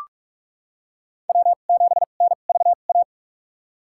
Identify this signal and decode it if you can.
Morse code, heard as a keyed tone, about 46 wpm, spaced as for 32 wpm, 700 Hz. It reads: W7NVA
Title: USA Call Signs - 5 Characters - 2x Character Spacing - 45wpm